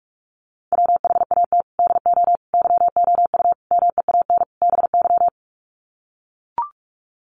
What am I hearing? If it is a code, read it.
Morse code, heard as a keyed tone, about 44 wpm, 700 Hz. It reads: WHAT DO YOU MEAN BY
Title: Sets of 5 Words 45wpm